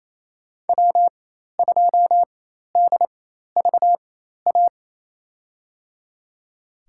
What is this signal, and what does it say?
Morse code, heard as a keyed tone, about 27 wpm, spaced as for 13 wpm, 700 Hz. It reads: W2DVA
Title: USA Call Signs - 5 Characters - 4x Character Spacing - 28wpm